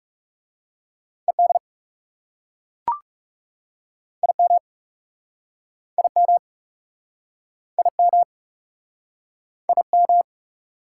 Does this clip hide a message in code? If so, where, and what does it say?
Yes: Morse code from 1.3 to 10.2 s, heard as a keyed tone, about 41 wpm, 700 Hz. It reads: ED IM IM IM IM